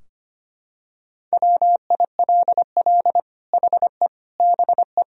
Morse code, heard as a keyed tone, about 25 wpm, 700 Hz. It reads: WILL HE BE
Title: Sentences from Top 1000 Words - Review Part 3 - 25wpm